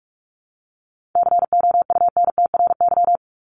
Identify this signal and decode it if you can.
Morse code, heard as a keyed tone, about 45 wpm, 700 Hz. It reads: COUNTRY
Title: Top 500 Words 45wpm